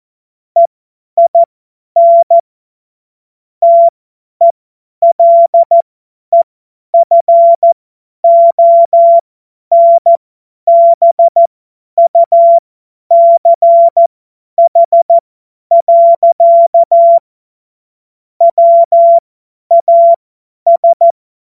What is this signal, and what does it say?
Morse code, heard as a keyed tone, about 14 wpm, spaced as for 10 wpm, 680 Hz. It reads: EIN TELEFONBUCH. WAS
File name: Morsekurs / Witze / Witz 14. Juni 2022 50 F70